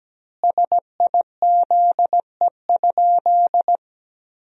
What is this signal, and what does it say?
Morse code, heard as a keyed tone, about 17 wpm, 700 Hz. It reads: SIZE?
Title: 2x Word-Spacing - Sentences from Top 500 Words - 17wpm